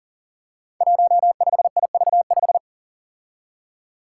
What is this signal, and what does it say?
Morse code, heard as a keyed tone, about 40 wpm, 700 Hz. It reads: 15IV5
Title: Five Challenging Letter-Number Combinations 40wpm